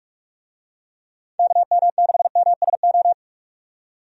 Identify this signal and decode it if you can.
Morse code, heard as a keyed tone, about 45 wpm, 700 Hz. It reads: KM6MSO
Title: USA Call Signs - 6 Characters - 45wpm